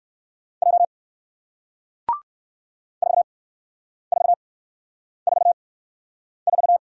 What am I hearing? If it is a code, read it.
Morse code, heard as a keyed tone, about 58 wpm, spaced as for 9 wpm, 700 Hz. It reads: J 4444